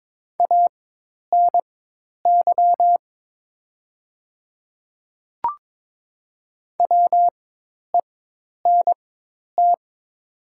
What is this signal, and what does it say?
Morse code, heard as a keyed tone, about 22 wpm, spaced as for 10 wpm, 700 Hz. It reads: ANY WENT
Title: 4x Character Spacing - Top 300 Words - 22wpm